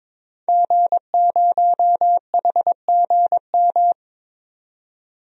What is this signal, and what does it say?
Morse code, heard as a keyed tone, about 22 wpm, 700 Hz. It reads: G0HGM
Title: DXCC-100 Call Signs - 5 Characters - 22wpm